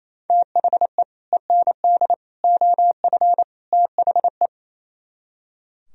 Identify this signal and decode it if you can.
Morse code, heard as a keyed tone, about 34 wpm, 690 Hz. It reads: THE END OF THE
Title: Speed-Racing - Sets of 4 Words 28wpm